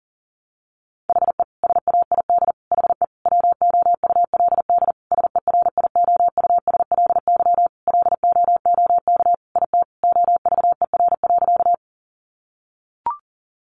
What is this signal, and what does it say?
Morse code, heard as a keyed tone, about 40 wpm, 700 Hz. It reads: HE SAID HE WOULD SERIOUSLY LOOK IT OVER.
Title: Sentences from Top 1200 Words 40wpm